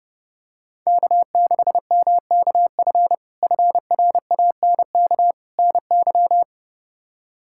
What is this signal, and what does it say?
Morse code, heard as a keyed tone, about 30 wpm, 700 Hz. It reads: K6MKF FRANK NY
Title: K1USN SST QSO Practice - 30wpm